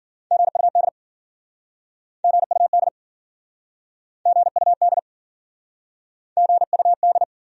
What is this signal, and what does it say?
Morse code, heard as a keyed tone, about 54 wpm, 700 Hz. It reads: GUD GUD GUD GUD